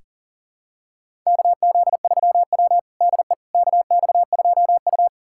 Morse code, heard as a keyed tone, about 40 wpm, 700 Hz. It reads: KZ3W DE KX2U
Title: USA Callsign DE Callsign - 4 Characters - 40wpm